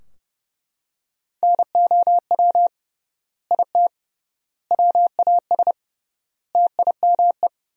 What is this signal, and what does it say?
Morse code, heard as a keyed tone, about 30 wpm, 700 Hz. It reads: NOW IT WAS TIME
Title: 3x Word-Spacing - Sentences from Top 200 Words - 30wpm